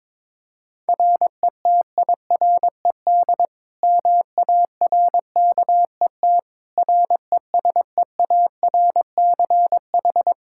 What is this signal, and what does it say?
Morse code, heard as a keyed tone, about 21 wpm, 700 Hz. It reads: RETIRED MARKET RESEARCH
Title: Practice QSO - Exchange 3 - More Info - 22wpm